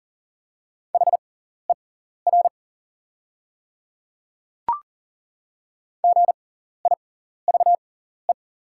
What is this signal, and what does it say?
Morse code, heard as a keyed tone, about 40 wpm, spaced as for 14 wpm, 700 Hz. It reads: HER GIVE
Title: 6x Character Spacing - Top 200 Words - 40wpm